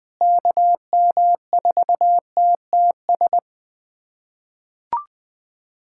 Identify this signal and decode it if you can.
Morse code, heard as a keyed tone, about 20 wpm, 700 Hz. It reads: KM4TTS
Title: USA Callsign DE Callsign - 6 Characters - 20wpm